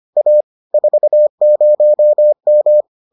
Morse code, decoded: A 40M

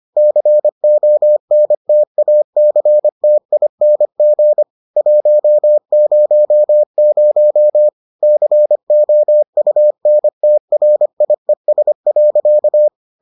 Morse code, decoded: CONTACTING 100 COUNTRIES.

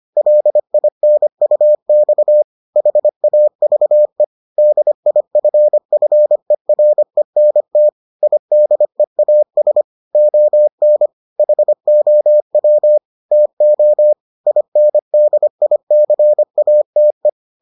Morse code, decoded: LINUX HAVE DIFFERENT IDEAS ON HOW TO INDICATE

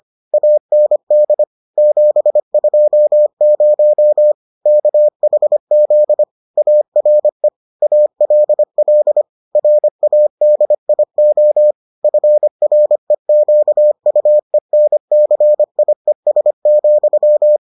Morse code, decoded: AND 720 KHZ ARE ALL RADIO FREQUENCIES,

